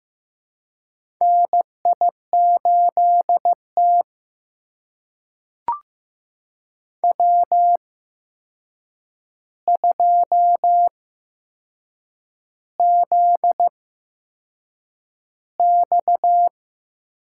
Morse code, heard as a keyed tone, about 15 wpm, spaced as for 6 wpm, 700 Hz. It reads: NI8T W2ZX
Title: USA Call Signs - 4 Characters - 8x Character Spacing - 15wpm